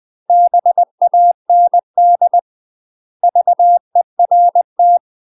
Morse code, heard as a keyed tone, about 20 wpm, 700 Hz. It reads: BAND VERT
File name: UK MASTER QSOs TRAINING / 20WPM / UK MASTER QSOs 0013